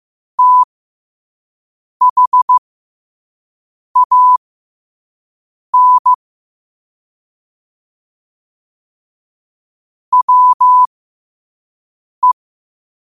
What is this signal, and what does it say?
Morse code, heard as a keyed tone, about 15 wpm, spaced as for 5 wpm, 1000 Hz. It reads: THAN WE
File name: qotdfiles / 5wpm